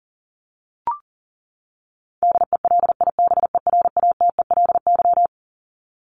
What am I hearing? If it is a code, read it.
Morse code, heard as a keyed tone, about 40 wpm, 700 Hz. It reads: DELIBERATELY
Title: Top 1751-2000 Words 40wpm